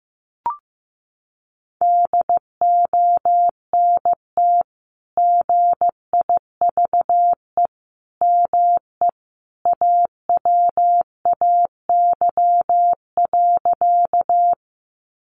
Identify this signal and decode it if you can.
Morse code, heard as a keyed tone, about 15 wpm, 700 Hz. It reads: DONT GIVE ME AWAY.
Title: Sentences from Top 600 Words 15wpm